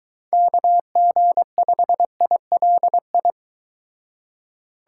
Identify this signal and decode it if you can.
Morse code, heard as a keyed tone, about 22 wpm, 700 Hz. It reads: KG5ILI